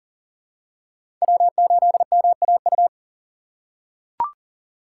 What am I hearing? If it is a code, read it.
Morse code, heard as a keyed tone, about 40 wpm, 700 Hz. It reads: W8MAU
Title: USA Call Signs - 5 Characters - 3x Character Spacing - 40wpm